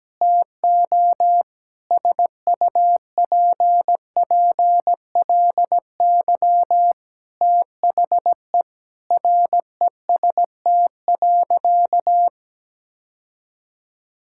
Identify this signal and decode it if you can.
Morse code, heard as a keyed tone, about 17 wpm, 700 Hz. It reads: TO SUPPLY THE REST.
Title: Sentences from Top 1000 Words - Review Part 5 - 17wpm